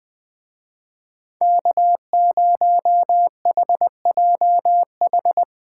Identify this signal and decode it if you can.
Morse code, heard as a keyed tone, about 22 wpm, 700 Hz. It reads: K0HJH
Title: Speed Racing - USA 1x3 Call Signs 20wpm